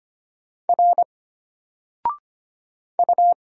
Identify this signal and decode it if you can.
Morse code, heard as a keyed tone, about 25 wpm, 700 Hz. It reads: R U
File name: Single-Letter-Number-25wpm